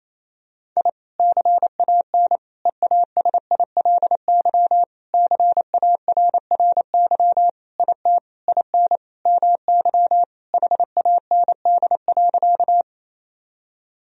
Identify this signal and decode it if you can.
Morse code, heard as a keyed tone, about 28 wpm, 700 Hz. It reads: I CAN EASILY CARRY IT IN MY HAND.
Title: Sentences from Top 1000 Words - Review Part 1 - 28wpm